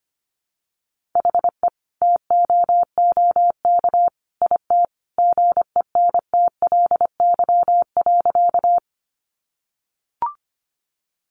Morse code, heard as a keyed tone, about 25 wpm, 700 Hz. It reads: HE TOOK IT GENTLY.
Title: Sentences from Top 1500 Words 25wpm